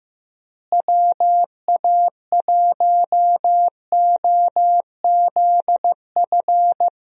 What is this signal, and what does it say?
Morse code, heard as a keyed tone, about 15 wpm, 700 Hz. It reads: WA1OZF
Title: USA Call Signs - 6 Characters - Repeated 3x - 15wpm